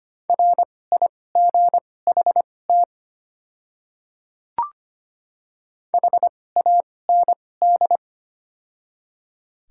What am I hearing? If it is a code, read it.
Morse code, heard as a keyed tone, about 25 wpm, spaced as for 18 wpm, 700 Hz. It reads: RIGHT HAND
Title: 2x Character Spacing - Top 300 Words - 25wpm